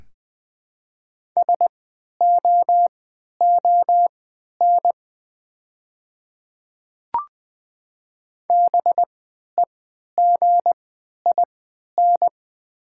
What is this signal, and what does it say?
Morse code, heard as a keyed tone, about 20 wpm, spaced as for 11 wpm, 700 Hz. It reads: SOON BEGIN
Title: 3x Character Spacing - Top 500 Words - 20wpm